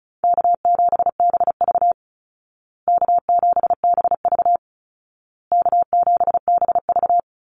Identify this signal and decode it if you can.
Morse code, heard as a keyed tone, about 34 wpm, 700 Hz. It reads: K7BV K7BV K7BV
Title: Easy Call Signs Repeated 3x - 35wpm